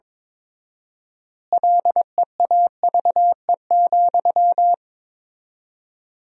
Morse code, heard as a keyed tone, about 22 wpm, 700 Hz. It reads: LEAVE,
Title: 4x Word-Spacing - Sentences from Top 400 Words - 22wpm